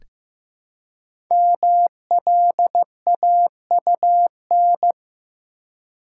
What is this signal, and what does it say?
Morse code, heard as a keyed tone, about 15 wpm, 700 Hz. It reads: MLAUN
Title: Code Groups - 5 Letters - 15wpm